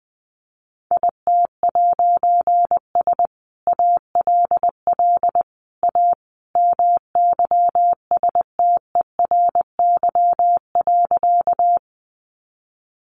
Morse code, heard as a keyed tone, about 20 wpm, 700 Hz. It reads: IT'S ALL A MYSTERY.
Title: A Sound Of Thunder 20wpm